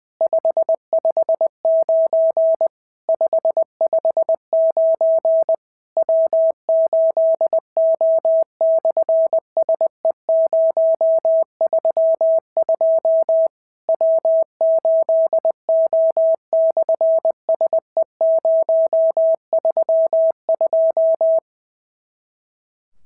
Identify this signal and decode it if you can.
Morse code, heard as a keyed tone, about 20 wpm, 650 Hz. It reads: 559 559 W8O/SE032 W8O/SE032
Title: SOTA QSO Practice 20wpm